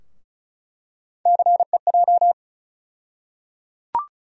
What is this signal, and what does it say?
Morse code, heard as a keyed tone, about 34 wpm, 700 Hz. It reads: CEJ
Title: USA Call Sign Postfixes - 3 Letters 35wpm